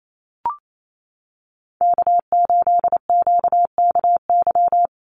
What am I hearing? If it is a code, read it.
Morse code, heard as a keyed tone, about 28 wpm, 700 Hz. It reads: K8QKY
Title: Call Signs - 28wpm